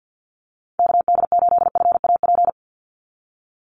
Morse code, heard as a keyed tone, about 50 wpm, 700 Hz. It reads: KD8FAP
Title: USA Call Signs 50wpm